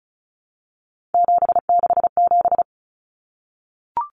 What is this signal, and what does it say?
Morse code, heard as a keyed tone, about 34 wpm, 700 Hz. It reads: 767
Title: Numbers - 3-Digit 35wpm